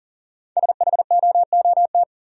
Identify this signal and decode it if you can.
Morse code, heard as a keyed tone, about 37 wpm, 700 Hz. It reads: SHOOT
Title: Sentences from Top 1200 Words - New Word Drill - 40wpm